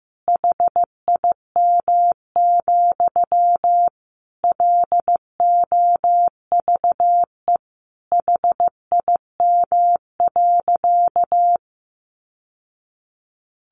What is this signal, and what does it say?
Morse code, heard as a keyed tone, about 15 wpm, 700 Hz. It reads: HIM, LOVE HIM.
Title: Sentences from Top 1100 Words 15wpm